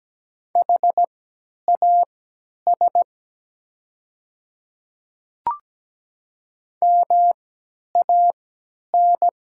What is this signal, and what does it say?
Morse code, heard as a keyed tone, about 17 wpm, spaced as for 10 wpm, 700 Hz. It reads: HAS MAN
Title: 3x Character Spacing - Top 200 Words - 17wpm